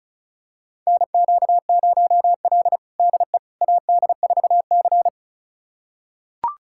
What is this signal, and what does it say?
Morse code, heard as a keyed tone, about 34 wpm, 700 Hz. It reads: NQ0L DE AD4C
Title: USA Callsign DE Callsign - 4 Characters - 35wpm